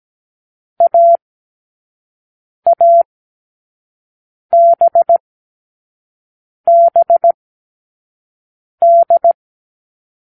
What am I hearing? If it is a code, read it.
Morse code, heard as a keyed tone, about 17 wpm, spaced as for 5 wpm, 680 Hz. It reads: AABBD
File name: k7qo / Lesson6(A-D)